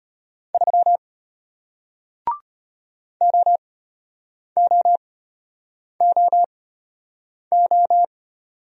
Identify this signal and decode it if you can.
Morse code, heard as a keyed tone, about 33 wpm, spaced as for 8 wpm, 700 Hz. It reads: 3 OOOO